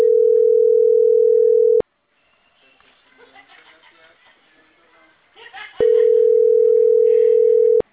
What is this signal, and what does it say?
Ringback tone (North America), 440 + 480 Hz, with speech.
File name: CENTRAL CAEA / out agcaea02